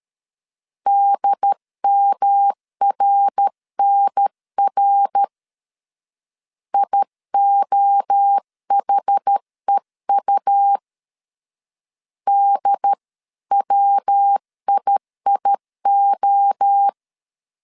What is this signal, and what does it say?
Morse code, heard as a keyed tone, about 13 wpm, 800 Hz. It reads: DMRNR IOHEU DWIIO
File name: CW / DET / 1 3 1500C S A 12wpm 2905